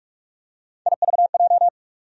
Morse code, heard as a keyed tone, about 43 wpm, 700 Hz. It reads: IVJ
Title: USA Call Sign Postfixes - 3 Letters 45wpm